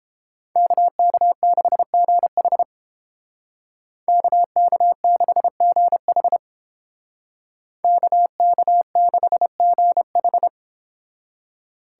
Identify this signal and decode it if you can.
Morse code, heard as a keyed tone, about 30 wpm, 700 Hz. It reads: KK6GH KK6GH KK6GH